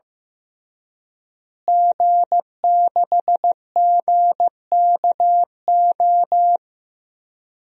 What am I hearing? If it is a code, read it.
Morse code, heard as a keyed tone, about 15 wpm, 700 Hz. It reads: G6GKO
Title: DXCC-100 Call Signs - 5 Characters - Repeated 3x - 15wpm